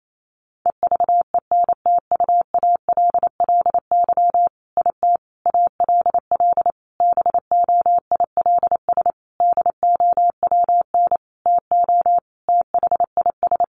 Morse code, heard as a keyed tone, about 28 wpm, 700 Hz. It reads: EVENTUALLY IT ALL BOILS DOWN TO THIS
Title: A Sound Of Thunder 28wpm